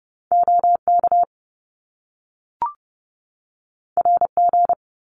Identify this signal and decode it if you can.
Morse code, heard as a keyed tone, about 30 wpm, 700 Hz. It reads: OK RG
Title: Top Two Letter Combinations in Words 30wpm